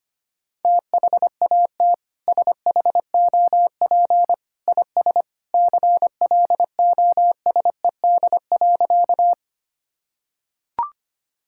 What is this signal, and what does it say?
Morse code, heard as a keyed tone, about 25 wpm, 700 Hz. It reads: THAT SHOP IS CLOSED.
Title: Sentences from Top 1600 Words 25wpm